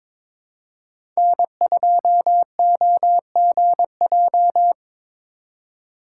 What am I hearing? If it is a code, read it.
Morse code, heard as a keyed tone, about 22 wpm, 690 Hz. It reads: N2OGJ
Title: USA Call Signs - 5 Characters - 2x Character Spacing - 22wpm